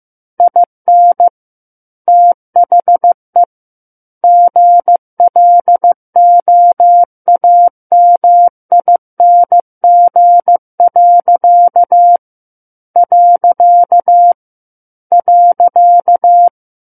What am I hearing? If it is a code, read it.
Morse code, heard as a keyed tone, about 15 wpm, 700 Hz. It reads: IN THE GLOAMING. .
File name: Chapter 6 - 15 WPM